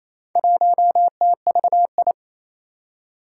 Morse code, heard as a keyed tone, about 27 wpm, 700 Hz. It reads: 1TVI
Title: Four Challenging Letter-Number Combinations 28wpm